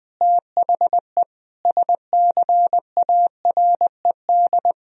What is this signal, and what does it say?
Morse code, heard as a keyed tone, about 20 wpm, 700 Hz. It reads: THE SCARED
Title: Sentences from Top 1000 Words - Review Part 4 - 20wpm